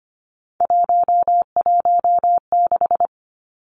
Morse code, heard as a keyed tone, about 25 wpm, 700 Hz. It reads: 116
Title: Three Challenging Letter-Number Combinations 25wpm